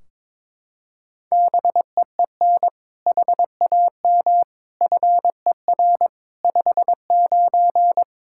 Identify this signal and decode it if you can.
Morse code, heard as a keyed tone, about 22 wpm, 700 Hz. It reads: BEEN HAM FER 59
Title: Practice QSO - Exchange 3 - More Info - 22wpm